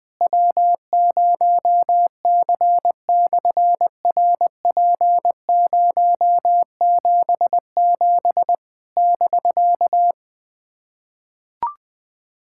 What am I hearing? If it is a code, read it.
Morse code, heard as a keyed tone, about 20 wpm, 700 Hz. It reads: W0C/RP077 <BK>
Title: SOTA QSO Practice 20wpm